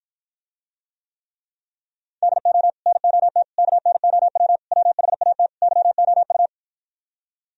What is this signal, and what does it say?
Morse code, heard as a keyed tone, about 53 wpm, 700 Hz. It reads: DO NOT KNOW WHAT YOU